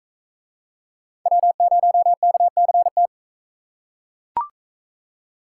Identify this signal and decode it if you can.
Morse code, heard as a keyed tone, about 38 wpm, 700 Hz. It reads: W0KCT